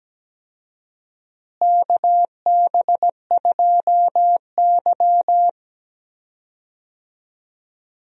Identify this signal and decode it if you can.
Morse code, heard as a keyed tone, about 17 wpm, 700 Hz. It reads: KB2Y
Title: USA Call Signs - 4 Characters - Repeated 3x - 17wpm